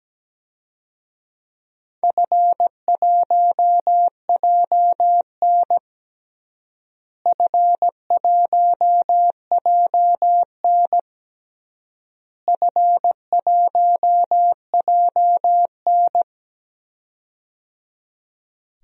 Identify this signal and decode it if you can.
Morse code, heard as a keyed tone, about 17 wpm, 700 Hz. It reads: F1JN F1JN F1JN
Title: DXCC-100 Call Signs - 4 Characters - Repeated 3x - 17wpm